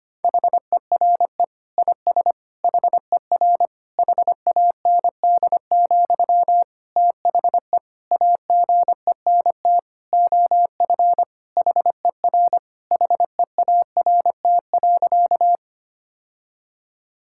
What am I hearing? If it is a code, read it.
Morse code, heard as a keyed tone, about 25 wpm, 700 Hz. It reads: HERE IS HER HAND, THE AGENT OF HER HEART.
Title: Sentences from Top 1600 Words 25wpm